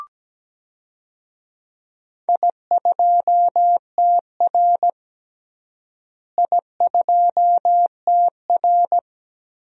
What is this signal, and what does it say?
Morse code, heard as a keyed tone, about 17 wpm, 700 Hz. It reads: I2TR I2TR
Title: DXCC-100 Call Signs - 4 Characters - Repeated 3x - 17wpm